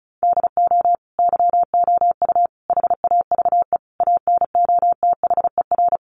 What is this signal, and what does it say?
Morse code, heard as a keyed tone, about 34 wpm, 700 Hz. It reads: DO YOU HAVE ANOTHER
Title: Sentences from Top 600 Words 35wpm